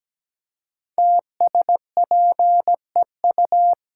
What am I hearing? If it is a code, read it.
Morse code, heard as a keyed tone, about 17 wpm, 700 Hz. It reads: TSPEU
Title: Code Groups - 5 Letter:Numbers - 17wpm